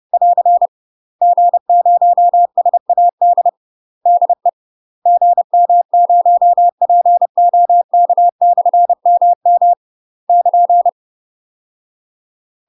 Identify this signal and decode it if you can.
Morse code, heard as a keyed tone, about 30 wpm, 700 Hz. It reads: <AR> G0SAD DE GM0POK/MM <KN>
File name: UK MASTER QSOs TRAINING / 30WPM / UK MASTER QSOs 0035